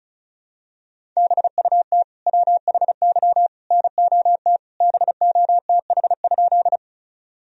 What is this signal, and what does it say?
Morse code, heard as a keyed tone, about 36 wpm, 700 Hz. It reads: BUT WHY NOT BOTH?
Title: Sentences from Top 1000 Words - Review Part 1 - 35wpm